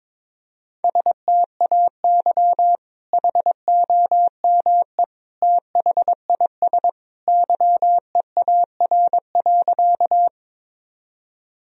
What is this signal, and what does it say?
Morse code, heard as a keyed tone, about 21 wpm, 700 Hz. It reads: STAY HOME THIS YEAR.
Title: Sentences from Top 1000 Words - Review Part 1 - 22wpm